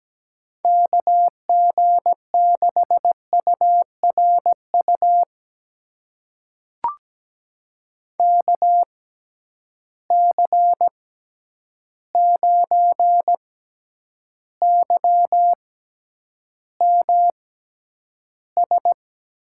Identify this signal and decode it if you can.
Morse code, heard as a keyed tone, about 17 wpm, spaced as for 9 wpm, 700 Hz. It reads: KG6URU KC9YMS
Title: USA Call Signs - 6 Characters - 6x Character Spacing - 17wpm